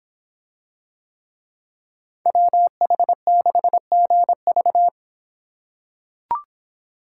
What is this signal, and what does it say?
Morse code, heard as a keyed tone, about 25 wpm, 700 Hz. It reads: WH6GV